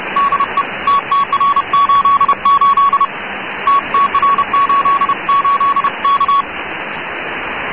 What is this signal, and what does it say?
Morse code, heard as a keyed tone, about 31 wpm, 1120 Hz, with radio noise. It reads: DE TTR88 TTR88 K
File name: africa / tcd / tcd-55-icrc n'djamena'ttr88'-cw,usb-clg+wkg-hbc88(1985)-(rb)-(mc-909-2-30)